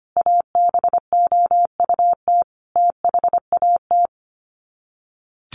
Morse code, heard as a keyed tone, about 25 wpm, 700 Hz. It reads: ABOUT THAT
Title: Sets of 5 Words Encore2 25wpm